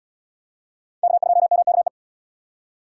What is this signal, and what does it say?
Morse code, heard as a keyed tone, about 77 wpm, 700 Hz. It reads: K2MKE